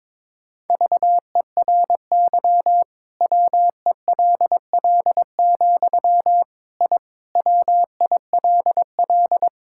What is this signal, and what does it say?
Morse code, heard as a keyed tone, about 22 wpm, 700 Hz. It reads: VERY WELL, I WILL
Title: Sentences from Top 1000 Words - Review Part 5 - 22wpm